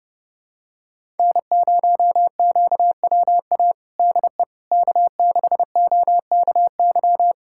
Morse code, heard as a keyed tone, about 32 wpm, 700 Hz. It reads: N0QWA DE K6OKY
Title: USA Callsign DE Callsign - 5 Characters - 30wpm